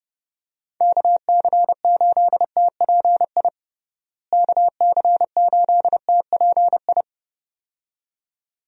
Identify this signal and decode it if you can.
Morse code, heard as a keyed tone, about 32 wpm, 700 Hz. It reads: KC8TPI KC8TPI
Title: USA Call Signs - 6 Characters - Repeated 2x - 30wpm